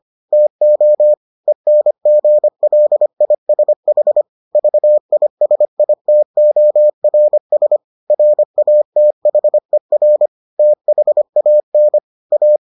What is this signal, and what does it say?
Morse code, heard as a keyed tone, about 25 wpm, 600 Hz. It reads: TO ENGLISH VISITORS RATHER THAN A